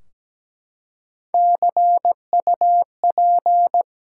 Morse code, heard as a keyed tone, about 17 wpm, 700 Hz. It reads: CUP
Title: Three Letter Words - 17wpm